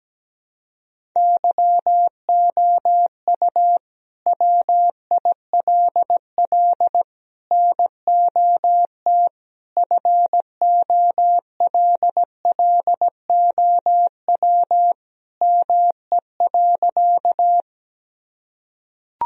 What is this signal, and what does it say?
Morse code, heard as a keyed tone, about 17 wpm, 700 Hz. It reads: YOU WILL NOT FOLLOW ME.
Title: Sentences from Top 1000 Words - Review Part 1 - 17wpm